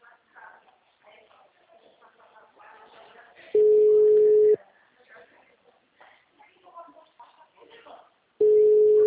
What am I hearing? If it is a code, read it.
Ringback tone (Europe, 425 Hz), with speech.